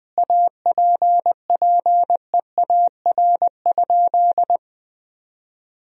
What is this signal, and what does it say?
Morse code, heard as a keyed tone, about 21 wpm, 700 Hz. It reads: APPEAR?
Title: Sentences from Top 1000 Words - Review Part 2 - 20wpm